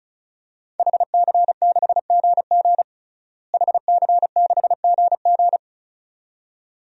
Morse code, heard as a keyed tone, about 36 wpm, 700 Hz. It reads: HC6GG HC6GG
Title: DXCC-100 Call Signs - 5 Characters - Repeated 2x - 35wpm